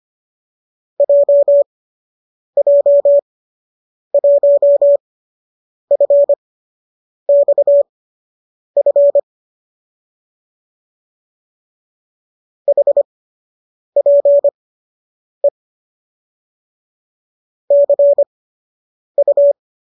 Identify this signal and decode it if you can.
Morse code, heard as a keyed tone, about 25 wpm, spaced as for 8 wpm, 570 Hz. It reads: JJ1FXF HPE CU